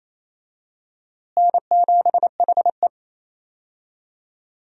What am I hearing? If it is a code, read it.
Morse code, heard as a keyed tone, about 28 wpm, 700 Hz. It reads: N7HE
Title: USA Call Signs - 4 Characters - Repeated 2x - 28wpm